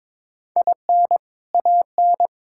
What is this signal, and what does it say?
Morse code, heard as a keyed tone, about 21 wpm, 700 Hz. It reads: IN AN
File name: practice / 22wpm / Sentences with 2-Word Phrases - 22wpm